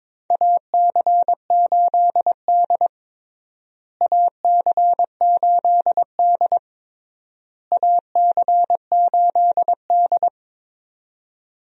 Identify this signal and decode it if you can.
Morse code, heard as a keyed tone, about 22 wpm, 700 Hz. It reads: AC8D AC8D AC8D
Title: USA Call Signs - 4 Characters - Repeated 3x - 22wpm